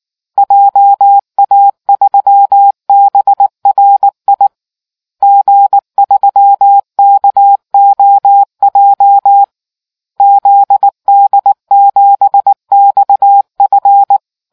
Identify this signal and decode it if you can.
Morse code, heard as a keyed tone, about 19 wpm, 800 Hz. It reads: JA3BRI G3KOJ ZD7XF